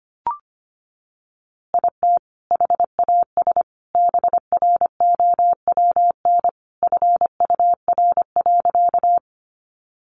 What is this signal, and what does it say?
Morse code, heard as a keyed tone, about 25 wpm, 700 Hz. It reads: IT HAS BROWN FUR.
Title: Easy Phrases 25wpm